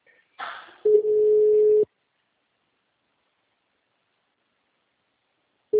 Ringback tone (Europe, 425 Hz).